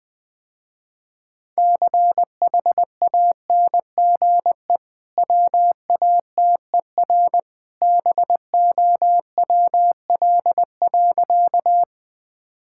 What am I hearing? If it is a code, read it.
Morse code, heard as a keyed tone, about 20 wpm, 700 Hz. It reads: CHANGE WATER BOWL.
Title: Easy Sentences with Hints - 20wpm